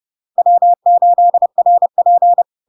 Morse code, heard as a keyed tone, about 30 wpm, 700 Hz. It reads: W8RP
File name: LARK-FD-2022-Log-in-CW-30wpm-700-Hz